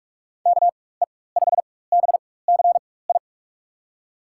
Morse code, heard as a keyed tone, about 46 wpm, spaced as for 21 wpm, 700 Hz. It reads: KE5BCI
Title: USA Call Signs - 6 Characters - 4x Character Spacing - 45wpm